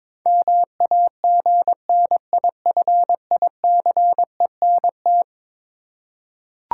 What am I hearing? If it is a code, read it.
Morse code, heard as a keyed tone, about 22 wpm, 700 Hz. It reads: MAGNIFICENT
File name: practice / 22wpm / Eleven Letter Words - 22wpm